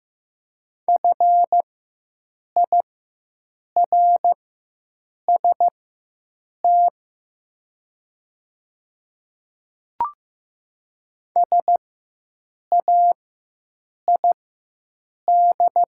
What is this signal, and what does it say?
Morse code, heard as a keyed tone, about 15 wpm, spaced as for 7 wpm, 700 Hz. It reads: FIRST SAID
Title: 4x Character Spacing - Top 100 Words - 15wpm